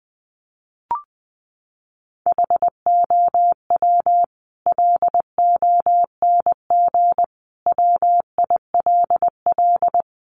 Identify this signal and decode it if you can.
Morse code, heard as a keyed tone, about 19 wpm, 700 Hz. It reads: HOW LONG WILL
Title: Sentences from Top 1200 Words 20wpm